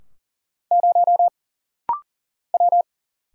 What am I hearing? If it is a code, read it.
Morse code, heard as a keyed tone, about 40 wpm, 700 Hz. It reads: 0 W
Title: Single Letter-Number Alt - Rapid-Fire 40wpm